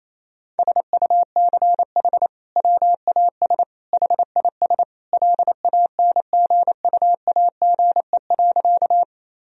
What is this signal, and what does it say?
Morse code, heard as a keyed tone, about 28 wpm, 700 Hz. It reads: SUCH WAS HIS LANGUAGE.
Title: Sentences from Top 1000 Words - Review Part 3 - 28wpm